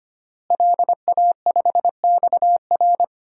Morse code, heard as a keyed tone, about 25 wpm, 700 Hz. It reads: LA5XR
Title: DXCC-100 Call Signs - 5 Characters - Repeated 3x - 25wpm